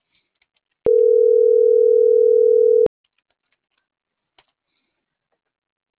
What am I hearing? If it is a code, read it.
Ringback tone (North America), 440 + 480 Hz.